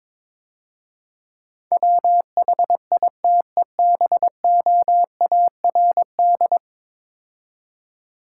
Morse code, heard as a keyed tone, about 22 wpm, 700 Hz. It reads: WHITEBOARD
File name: Easy Words - Speed-Racing with Hints - 22wpm